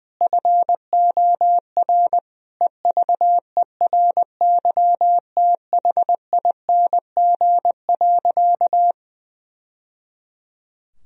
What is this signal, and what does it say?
Morse code, heard as a keyed tone, about 20 wpm, 700 Hz. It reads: FOR EVERYTHING.
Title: Sentences from Top 1000 Words - Review Part 3 - 20wpm